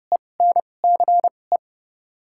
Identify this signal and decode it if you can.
Morse code, heard as a keyed tone, about 30 wpm, spaced as for 22 wpm, 700 Hz. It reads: ENCE
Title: 2x Character Spacing - Top 500 Words - 30wpm